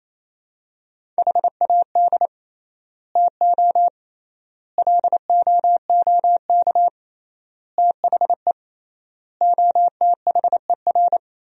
Morse code, heard as a keyed tone, about 28 wpm, 700 Hz. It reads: HAD TO LOOK THE OTHER
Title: 3x Word-Spacing - Sentences from Top 100 Words - 28wpm